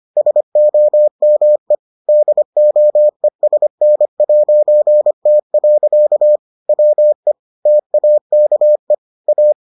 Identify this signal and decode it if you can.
Morse code, heard as a keyed tone, about 25 wpm, 600 Hz. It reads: SOME DOESN'T. WE TAKE A